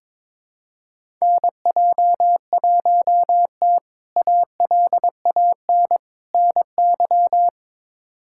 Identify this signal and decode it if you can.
Morse code, heard as a keyed tone, about 22 wpm, 700 Hz. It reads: NJ1T ALAN NY
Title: K1USN SST QSO Practice - 22wpm